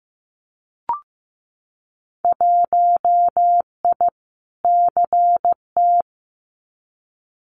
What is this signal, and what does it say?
Morse code, heard as a keyed tone, about 15 wpm, 700 Hz. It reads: 1I CT
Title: Winter Field Day Exchanges 15wpm